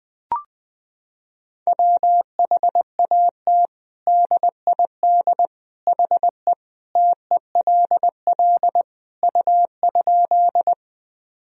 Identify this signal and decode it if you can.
Morse code, heard as a keyed tone, about 20 wpm, 700 Hz. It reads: WHAT DID HE TELL U?
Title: Sentences from Top 200 Words and Abbreviations 20wpm